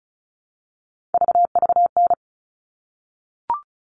Morse code, heard as a keyed tone, about 34 wpm, 700 Hz. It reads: VVN
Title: All Cut Numbers - 3-Digit 35wpm